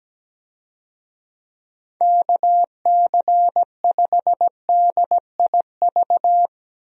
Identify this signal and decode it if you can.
Morse code, heard as a keyed tone, about 17 wpm, 700 Hz. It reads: KC5DIV
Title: USA Call Signs - 6 Characters - Repeated 2x - 17wpm